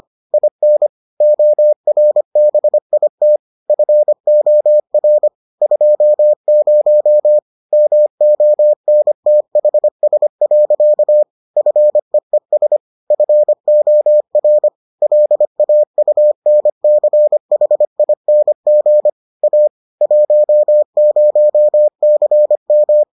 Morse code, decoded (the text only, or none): IN ORBIT FOR 20 MONTHS. FEES FOR LAUNCHING A 10CM